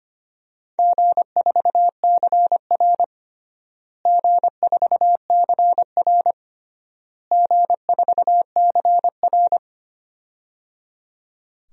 Morse code, heard as a keyed tone, about 25 wpm, 700 Hz. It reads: G4CR G4CR G4CR
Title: DXCC-100 Call Signs - 4 Characters - Repeated 3x - 25wpm